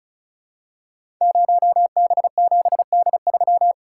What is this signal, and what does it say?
Morse code, heard as a keyed tone, about 36 wpm, 700 Hz. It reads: 0B7D3
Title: Code Groups - 5 Letter:Numbers - 35wpm